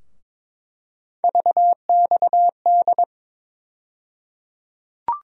Morse code, heard as a keyed tone, about 22 wpm, 700 Hz. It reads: VXD
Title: USA Call Sign Postfixes - 3 Letters 22wpm